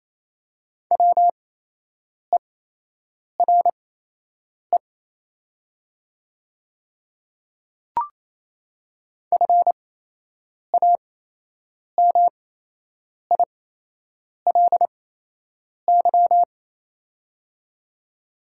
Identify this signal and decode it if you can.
Morse code, heard as a keyed tone, about 28 wpm, spaced as for 8 wpm, 700 Hz. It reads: WERE FAMILY
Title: 8x Character Spacing - Top 500 Words - 28wpm